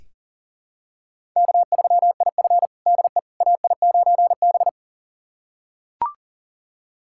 Morse code, heard as a keyed tone, about 40 wpm, 700 Hz. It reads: K3IF DE AI9B
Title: USA Callsign DE Callsign - 4 Characters - 40wpm